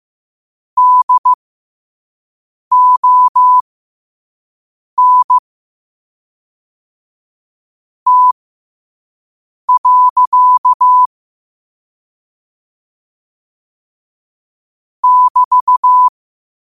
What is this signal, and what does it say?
Morse code, heard as a keyed tone, about 15 wpm, spaced as for 5 wpm, 1000 Hz. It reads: DON T. <BT>
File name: qotdfiles / 5wpm